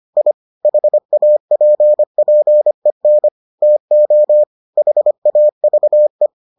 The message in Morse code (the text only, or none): I HAPPEN TO HAVE